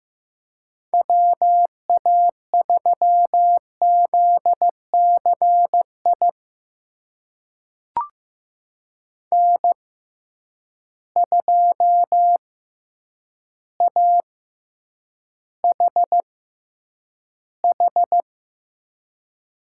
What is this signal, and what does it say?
Morse code, heard as a keyed tone, about 15 wpm, spaced as for 8 wpm, 700 Hz. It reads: WA3ZCI N2AHH
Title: USA Call Signs - 6x Character Spacing - 15wpm